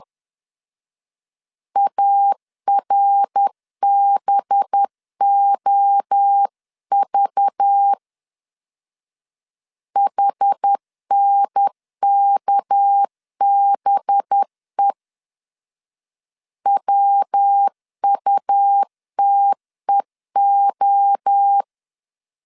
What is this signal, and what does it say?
Morse code, heard as a keyed tone, about 10 wpm, 800 Hz. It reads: ARBOV HNKBE WUTEO
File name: LEZIONI / 6 PKX0 / 6 PK 10wpm 2436